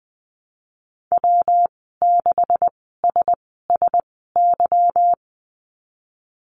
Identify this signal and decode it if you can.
Morse code, heard as a keyed tone, about 20 wpm, spaced as for 15 wpm, 700 Hz. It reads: W6SSY
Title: 207 Lesson 33g - Letter j - Call Signs 20wpm